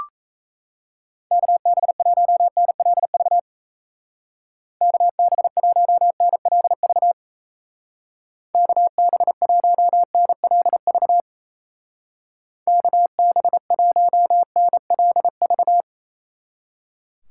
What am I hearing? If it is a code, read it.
Morse code, heard as a keyed tone, about 34 wpm, 700 Hz. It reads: KB1NLV KB1NLV KB1NLV KB1NLV
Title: Speed Racing - USA Call Signs 28wpm